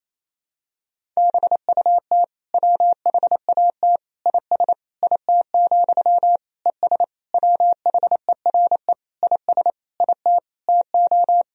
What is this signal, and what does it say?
Morse code, heard as a keyed tone, about 28 wpm, 700 Hz. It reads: BUT WHAT IS IT, ES WHERE IS IT TO
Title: Sentences from Top 300 Words and Abbreviations 28wpm